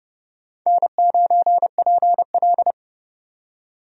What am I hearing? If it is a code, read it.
Morse code, heard as a keyed tone, about 31 wpm, 700 Hz. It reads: N9PL
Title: USA Call Signs - 4 Characters - 30wpm